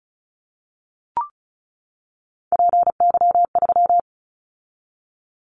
Morse code, heard as a keyed tone, about 36 wpm, 700 Hz. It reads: PY3
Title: Callsign Prefixes for Top 50 DXCC - 35wpm